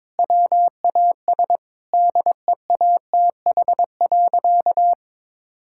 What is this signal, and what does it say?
Morse code, heard as a keyed tone, about 22 wpm, 700 Hz. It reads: WAS DEATH.
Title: Sentences from Top 1600 Words 22wpm